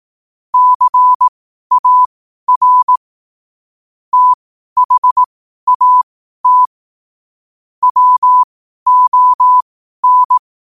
Morse code, heard as a keyed tone, about 18 wpm, spaced as for 12 wpm, 1000 Hz. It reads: CAR THAT WON
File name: qotdfiles / 12wpm